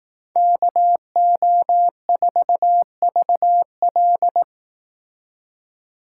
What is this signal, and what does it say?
Morse code, heard as a keyed tone, about 18 wpm, 700 Hz. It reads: KO4VL
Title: Speed Racing - USA 2x2 Call Signs 15wpm